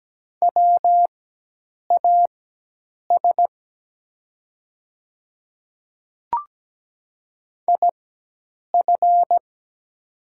Morse code, heard as a keyed tone, about 17 wpm, spaced as for 8 wpm, 700 Hz. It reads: WAS IF
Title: 4x Character Spacing - Top 100 Words - 17wpm